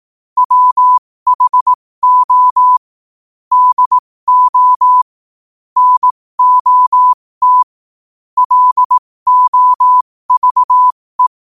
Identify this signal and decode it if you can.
Morse code, heard as a keyed tone, about 18 wpm, spaced as for 15 wpm, 1000 Hz. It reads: WHO DO NOT LOVE
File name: qotdfiles / 15wpm